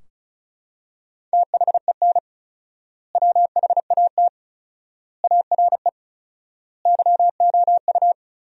Morse code, heard as a keyed tone, about 36 wpm, 700 Hz. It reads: THEN WHAT ARE YOU
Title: 4x Word-Spacing - Sentences from Top 100 Words - 35wpm